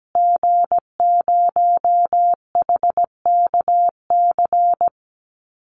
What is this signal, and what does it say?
Morse code, heard as a keyed tone, about 17 wpm, 700 Hz. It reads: G0HKC
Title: Call Signs Repeated 2x - 17wpm